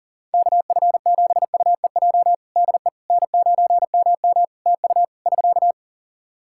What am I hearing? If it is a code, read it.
Morse code, heard as a keyed tone, about 37 wpm, 700 Hz. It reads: KF7UEJ DE N9MM TU <SK>
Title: Practice QSO - Exchange 4 - Ending - 40wpm